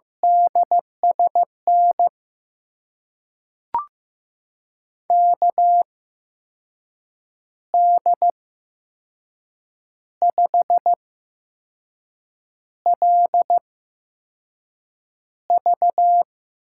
Morse code, heard as a keyed tone, about 15 wpm, spaced as for 5 wpm, 700 Hz. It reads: DSN KD5LV
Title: USA Call Signs - 5 Characters - 8x Character Spacing - 15wpm